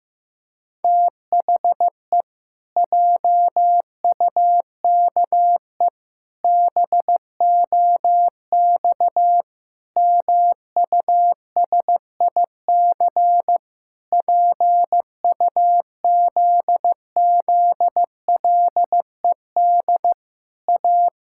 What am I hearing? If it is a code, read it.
Morse code, heard as a keyed tone, about 15 wpm, 700 Hz. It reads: THE JUKE BOX MUSIC PUZZLED A
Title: Pangram Challenge v2 - 15wpm